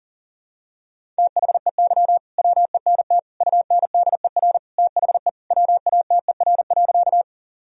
Morse code, heard as a keyed tone, about 37 wpm, 700 Hz. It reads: THEY WENT UNDER THE WATER.
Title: Sentences from Top 1000 Words - Review Part 1 - 40wpm